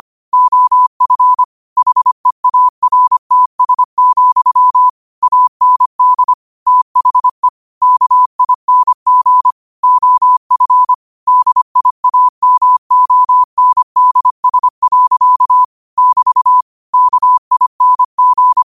Morse code, heard as a keyed tone, about 25 wpm, 1000 Hz. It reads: OF HEARTS, AND THE KING OF DIAMONDS. <BT> KING
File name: qotdfiles / 25wpm